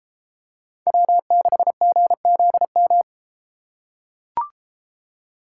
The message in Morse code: W6GZM

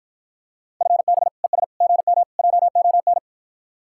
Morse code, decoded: FB ES GM JON

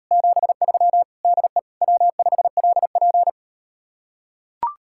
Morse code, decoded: 73 DE W5LP

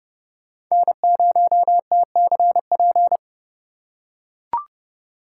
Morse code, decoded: N0TCP

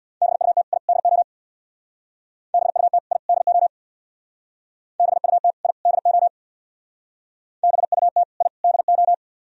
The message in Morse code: BUT I DO BUT I DO BUT I DO BUT I DO

